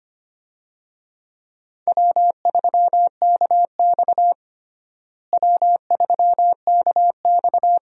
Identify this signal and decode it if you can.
Morse code, heard as a keyed tone, about 25 wpm, 700 Hz. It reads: W3KX W3KX
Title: USA Call Signs - 4 Characters - Repeated 2x - 25wpm